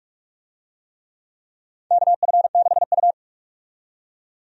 Morse code, heard as a keyed tone, about 46 wpm, 700 Hz. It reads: KF6U
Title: USA Call Signs - 4 Characters - 45wpm